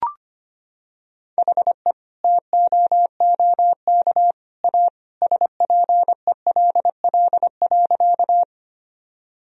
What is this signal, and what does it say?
Morse code, heard as a keyed tone, about 25 wpm, 700 Hz. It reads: HE TOOK A SPELL.
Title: Sentences from Top 1000 Words - Review Part 1 - 25wpm